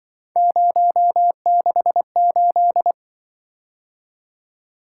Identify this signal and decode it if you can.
Morse code, heard as a keyed tone, about 24 wpm, 700 Hz. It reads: 068
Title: Speed-Racing - Three Letter-Number Combinations - 20wpm